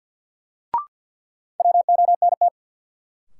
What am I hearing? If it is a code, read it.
Morse code, heard as a keyed tone, about 53 wpm, 700 Hz. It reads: WONT
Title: Sentences from Top 1000 Words - Review Part 6 - New Word Drill - 50wpm